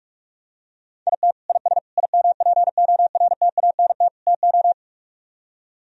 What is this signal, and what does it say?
Morse code, heard as a keyed tone, about 44 wpm, 700 Hz. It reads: IT IS IMPORTANT TO